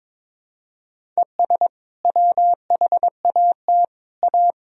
Morse code, heard as a keyed tone, about 22 wpm, 700 Hz. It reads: ES WHAT A
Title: Sentences from Top 500 Words and Abbreviations 22wpm